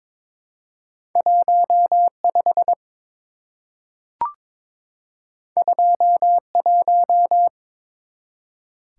Morse code, heard as a keyed tone, about 22 wpm, 700 Hz. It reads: 15 21
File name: practice / 22wpm / Two Letter-Number Combinations - 22wpm